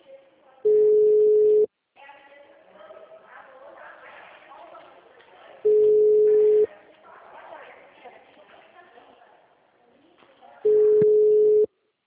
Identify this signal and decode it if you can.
Ringback tone (Europe, 425 Hz), with speech.